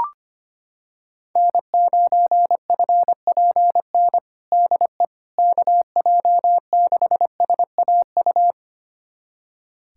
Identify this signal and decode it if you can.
Morse code, heard as a keyed tone, about 25 wpm, 700 Hz. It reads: N9FPN DE KJ6SAU
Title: USA Callsign DE Callsign 25wpm